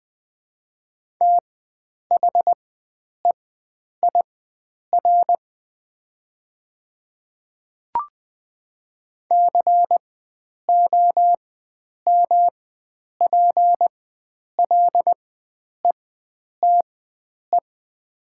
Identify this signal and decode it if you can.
Morse code, heard as a keyed tone, about 21 wpm, spaced as for 9 wpm, 700 Hz. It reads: THEIR COMPLETE
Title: 4x Character Spacing - Top 500 Words - 20wpm